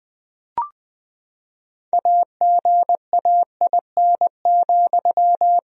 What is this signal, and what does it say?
Morse code, heard as a keyed tone, about 20 wpm, 700 Hz. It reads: AGAIN,
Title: Sentences from Top 1000 Words - Review Part 3 - 20wpm